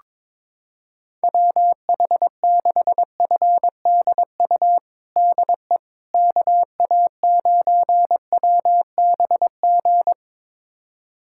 Morse code, heard as a keyed tone, about 21 wpm, 700 Hz. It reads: WH6FDU DE KA9WBG
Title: USA Callsign DE Callsign - 6 Characters - 22wpm